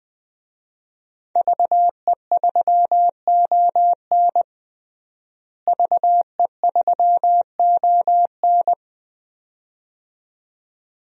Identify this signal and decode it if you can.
Morse code, heard as a keyed tone, about 21 wpm, 700 Hz. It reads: VE3ON VE3ON
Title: DXCC-100 Call Signs - 5 Characters - Repeated 2x - 20wpm